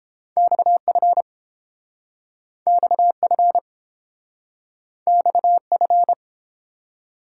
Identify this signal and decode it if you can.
Morse code, heard as a keyed tone, about 30 wpm, 700 Hz. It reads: XF XF XF